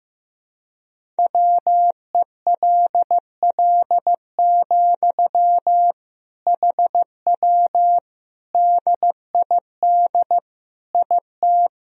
Morse code, heard as a keyed tone, about 15 wpm, 700 Hz. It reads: WELL, HW DID IT
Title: Sentences from Top 200 Words and Abbreviations 15wpm